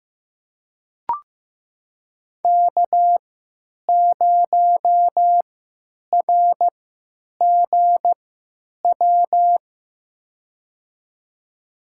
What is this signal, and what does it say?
Morse code, heard as a keyed tone, about 15 wpm, spaced as for 9 wpm, 700 Hz. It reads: K0RGW
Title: USA Call Signs - 5 Characters - 3x Character Spacing - 15wpm